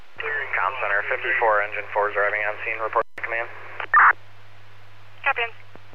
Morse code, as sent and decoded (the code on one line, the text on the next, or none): none